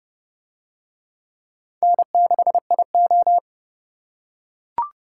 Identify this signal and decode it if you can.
Morse code, heard as a keyed tone, about 31 wpm, 700 Hz. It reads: N6IO